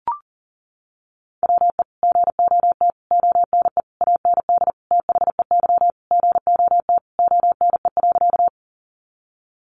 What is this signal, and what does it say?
Morse code, heard as a keyed tone, about 40 wpm, 700 Hz. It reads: WE GOT ONE AND THEY GOT ONE.
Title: Sentences from Top 300 Words 40wpm